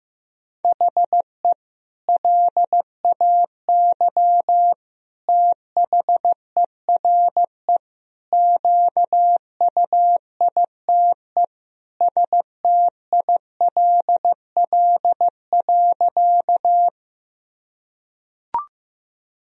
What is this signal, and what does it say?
Morse code, heard as a keyed tone, about 15 wpm, 700 Hz. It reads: HE LAY THERE QUITE STILL.
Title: Sentences from Top 1000 Words - Review Part 2 - 15wpm